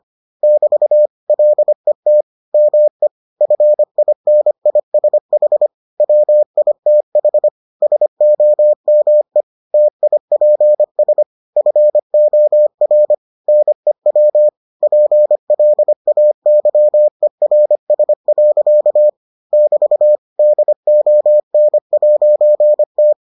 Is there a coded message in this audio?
Yes: Morse code from 0.4 to 23.1 s, heard as a keyed tone, about 25 wpm, 600 Hz. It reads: <BT> LET ME FINISH WITH SOME TIPS FOR NEW PLAYERS. <BT> DON'T